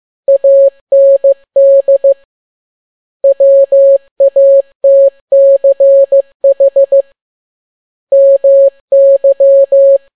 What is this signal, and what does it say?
Morse code, heard as a keyed tone, about 15 wpm, 550 Hz. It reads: AND WATCH MY
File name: n6tt p1 15C13